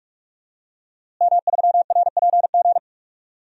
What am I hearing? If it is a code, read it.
Morse code, heard as a keyed tone, about 46 wpm, 700 Hz. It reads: M3RPG
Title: DXCC-100 Call Signs - 5 Characters - Repeated 2x - 45wpm